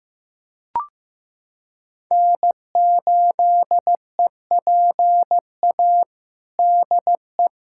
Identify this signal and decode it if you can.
Morse code, heard as a keyed tone, about 15 wpm, 700 Hz. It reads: N8EPA DE
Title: USA Callsign DE Callsign - 5 Characters - 15wpm